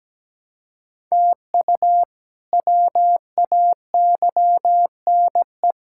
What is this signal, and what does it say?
Morse code, heard as a keyed tone, about 17 wpm, 700 Hz. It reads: TU WAYNE
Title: K1USN SST QSO Practice - 17wpm